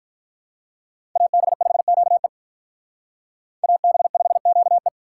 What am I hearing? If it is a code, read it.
Morse code, heard as a keyed tone, about 45 wpm, 700 Hz. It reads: AB5QE AB5QE